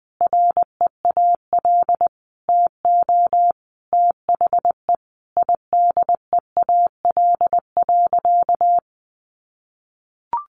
Morse code, heard as a keyed tone, about 20 wpm, 700 Hz. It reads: REAL TO THE IDEAL.
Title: Sentences from Top 1400 Words 20wpm